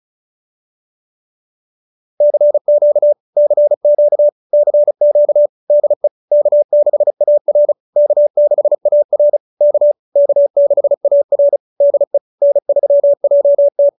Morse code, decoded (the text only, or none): CQ CQ CQ DE K6AR K6AR K K6AR DE N3JT